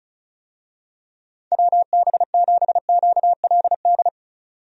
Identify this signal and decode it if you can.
Morse code, heard as a keyed tone, about 34 wpm, 700 Hz. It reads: WB7QLD
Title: USA Call Signs - 6 Characters - 35wpm